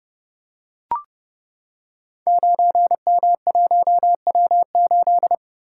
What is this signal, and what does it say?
Morse code, heard as a keyed tone, about 29 wpm, 700 Hz. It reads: 9M1W8
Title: Code Groups - 5 Letter:Numbers - 30wpm